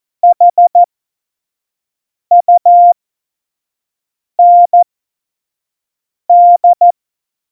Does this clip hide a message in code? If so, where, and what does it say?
Yes: Morse code from 0.2 to 6.9 s, heard as a keyed tone, about 14 wpm, spaced as for 5 wpm, 700 Hz. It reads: HUND